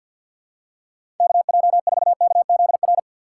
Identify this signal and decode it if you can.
Morse code, heard as a keyed tone, about 48 wpm, 700 Hz. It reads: KJ4KZR
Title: USA Call Signs - 6 Characters - 6x Character Spacing - 50wpm